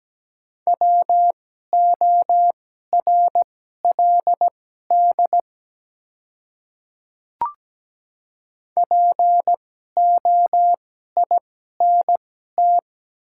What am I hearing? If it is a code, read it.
Morse code, heard as a keyed tone, about 17 wpm, spaced as for 12 wpm, 690 Hz. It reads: WORLD POINT
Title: 2x Character Spacing - Top 300 Words - 17wpm